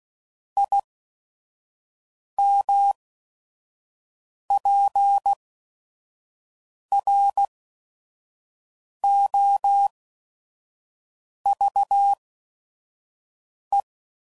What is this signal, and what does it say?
Morse code, heard as a keyed tone, about 16 wpm, spaced as for 5 wpm, 800 Hz. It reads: IMPROVE